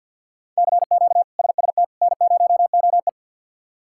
Morse code, heard as a keyed tone, about 51 wpm, 700 Hz. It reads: CQ SST N0OE
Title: K1USN SST QSO Practice - 50wpm